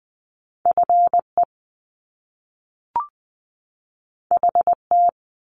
Morse code, heard as a keyed tone, about 20 wpm, 700 Hz. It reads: FE HT
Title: Top Two Letter Combinations in Words 20wpm